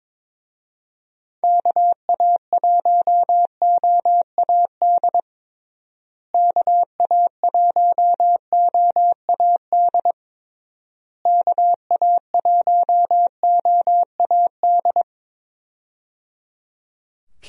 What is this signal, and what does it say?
Morse code, heard as a keyed tone, about 22 wpm, 700 Hz. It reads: KA1OAD KA1OAD KA1OAD
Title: USA Call Signs - 6 Characters - Repeated 3x - 22wpm